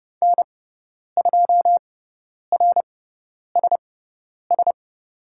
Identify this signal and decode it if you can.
Morse code, heard as a keyed tone, about 29 wpm, spaced as for 10 wpm, 700 Hz. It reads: N2RSS
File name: Lesson 16g - Letter c - Call Signs 30wpm